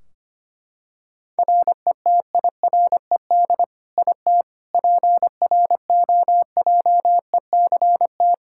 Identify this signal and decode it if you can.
Morse code, heard as a keyed tone, about 25 wpm, 700 Hz. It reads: RETIRED IT PROJECT
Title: Practice QSO - Exchange 3 - More Info - 25wpm